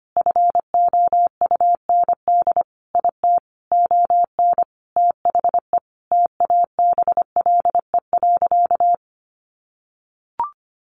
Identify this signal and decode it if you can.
Morse code, heard as a keyed tone, about 25 wpm, 700 Hz. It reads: FOUND IT ON THE TABLE.
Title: Sentences from Top 1500 Words 25wpm